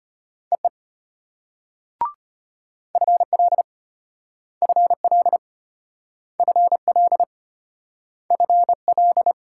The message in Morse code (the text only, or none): EE FL FL FL FL